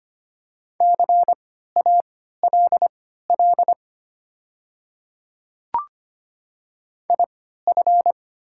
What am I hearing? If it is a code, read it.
Morse code, heard as a keyed tone, about 25 wpm, spaced as for 14 wpm, 700 Hz. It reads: CALL IF
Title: 3x Character Spacing - Top 100 Words - 25wpm